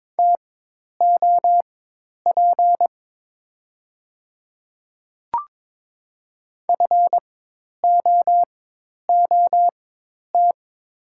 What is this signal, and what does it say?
Morse code, heard as a keyed tone, about 22 wpm, spaced as for 10 wpm, 700 Hz. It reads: TOP FOOT
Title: 4x Character Spacing - Top 500 Words - 22wpm